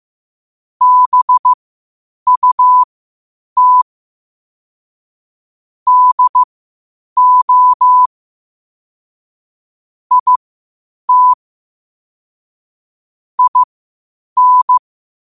Morse code, heard as a keyed tone, about 15 wpm, spaced as for 8 wpm, 1000 Hz. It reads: BUT DO IT IN